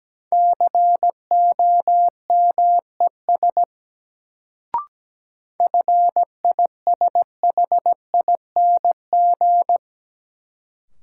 Morse code, heard as a keyed tone, about 17 wpm, 700 Hz. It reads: COMES FISHING
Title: Sentences from Top 1000 Words - Review Part 5 - New Word Drill - 17wpm